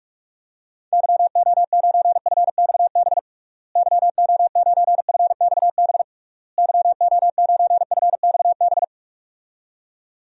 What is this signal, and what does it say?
Morse code, heard as a keyed tone, about 46 wpm, 700 Hz. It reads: YO9FXB YO9FXB YO9FXB
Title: DXCC-100 Call Signs - 6 Characters - Repeated 3x - 45wpm